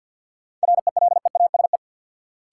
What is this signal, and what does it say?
Morse code, heard as a keyed tone, about 52 wpm, 700 Hz. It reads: RELEASE
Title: Sentences from Top 1100 Words - New Word Drill - 50wpm